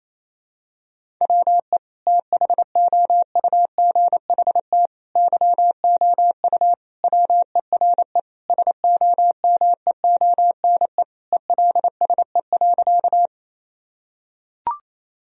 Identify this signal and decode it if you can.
Morse code, heard as a keyed tone, about 28 wpm, 700 Hz. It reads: WE THOUGHT YOU WERE SOMEONE ELSE.
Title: Sentences from Top 1000 Words - Review Part 4 - 28wpm